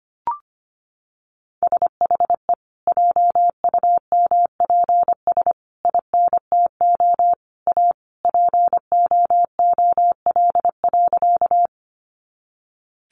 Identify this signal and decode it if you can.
Morse code, heard as a keyed tone, about 25 wpm, 700 Hz. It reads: SHE JUMPS INTO A POOL.
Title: Additional Easy Phrases 25wpm